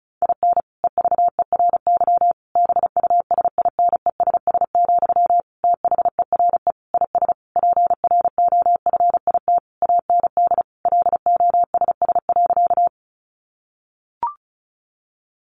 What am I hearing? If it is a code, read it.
Morse code, heard as a keyed tone, about 34 wpm, 700 Hz. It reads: IN EVERY BUSINESS, THERE IS PROFIT AND LOSS.
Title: Sentences from Top 1400 Words 35wpm